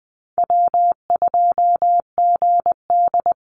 Morse code, heard as a keyed tone, about 20 wpm, 700 Hz. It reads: W2GD
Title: Easy Call Signs - 20wpm